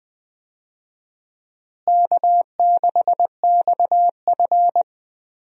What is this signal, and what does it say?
Morse code, heard as a keyed tone, about 20 wpm, 700 Hz. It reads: K6XF
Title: USA Call Signs - 4 Characters - Repeated 2x - 20wpm